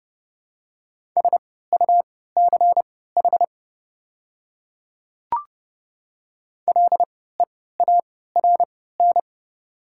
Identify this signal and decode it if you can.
Morse code, heard as a keyed tone, about 30 wpm, spaced as for 17 wpm, 690 Hz. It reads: SUCH LEARN
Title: 3x Character Spacing - Top 300 Words - 30wpm